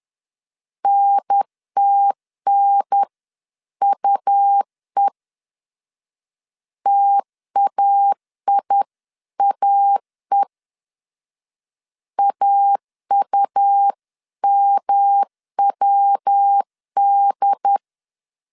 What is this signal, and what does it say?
Morse code, heard as a keyed tone, about 10 wpm, 800 Hz. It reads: NTNUE TAIAE AUMWD
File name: CW / DET / 1 3 1500C S C 10wpm 3335